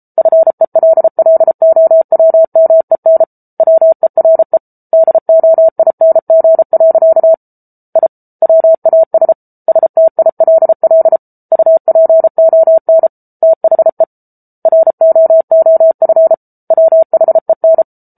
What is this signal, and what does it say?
Morse code, heard as a keyed tone, about 34 wpm, 650 Hz. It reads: FELLOWMEN WERE DOING. I WAS STILL UPON THE ROOF WHEN